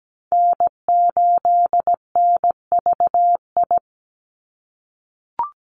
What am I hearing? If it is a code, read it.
Morse code, heard as a keyed tone, about 17 wpm, 700 Hz. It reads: N8NVI
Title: USA Call Signs 17wpm